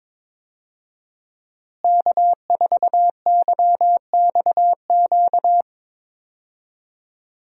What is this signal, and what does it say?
Morse code, heard as a keyed tone, about 22 wpm, 700 Hz. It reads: K4YXQ
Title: USA Call Signs - 5 Characters - Repeated 3x - 22wpm